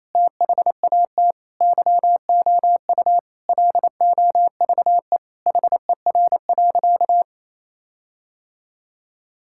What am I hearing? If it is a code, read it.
Morse code, heard as a keyed tone, about 28 wpm, 700 Hz. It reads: THAT YOU LOVE HER.
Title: Sentences from Top 1600 Words 28wpm